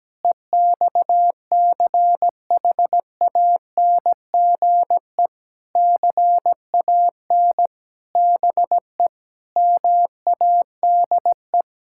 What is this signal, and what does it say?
Morse code, heard as a keyed tone, about 17 wpm, 700 Hz. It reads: EXCHANGE CAN BE MADE
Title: Sentences from Top 1000 Words - Review Part 6 - 17wpm